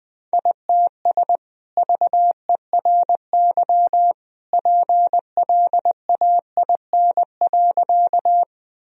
Morse code, heard as a keyed tone, about 20 wpm, 700 Hz. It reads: ITS VERY PLAIN.
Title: Sentences from Top 1000 Words - Review Part 2 - 20wpm